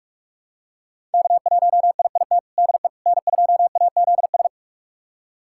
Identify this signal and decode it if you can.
Morse code, heard as a keyed tone, about 41 wpm, 700 Hz. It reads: K1IIT DE N2AZS
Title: USA Callsign DE Callsign - 5 Characters - 45wpm